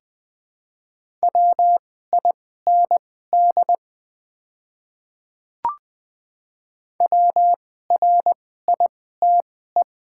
Morse code, heard as a keyed tone, about 20 wpm, spaced as for 14 wpm, 700 Hz. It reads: WIND WRITE
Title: 2x Character Spacing - Top 500 Words - 20wpm